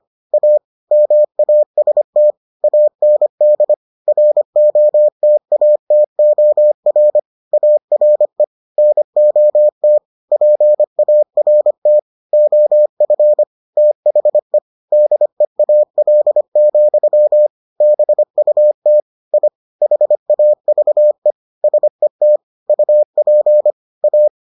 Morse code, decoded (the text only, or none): A MAST AND ROTATOR ARE NOT PART OF THE DEAL, BUT I HAVE SET UP A